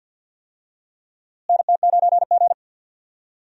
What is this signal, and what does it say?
Morse code, heard as a keyed tone, about 46 wpm, 700 Hz. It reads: NT9G
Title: USA Call Signs - 4 Characters - Repeated 2x - 50wpm